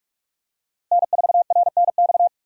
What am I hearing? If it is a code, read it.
Morse code, heard as a keyed tone, about 46 wpm, 700 Hz. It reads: N4RNX
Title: USA Call Signs - 5 Characters - Repeated 2x - 45wpm